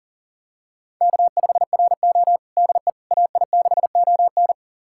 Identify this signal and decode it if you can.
Morse code, heard as a keyed tone, about 40 wpm, 700 Hz. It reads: K5RO DE AI6ON
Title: USA Callsign DE Callsign 40wpm